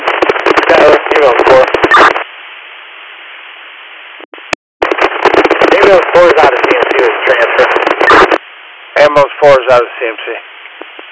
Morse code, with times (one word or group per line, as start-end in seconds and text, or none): none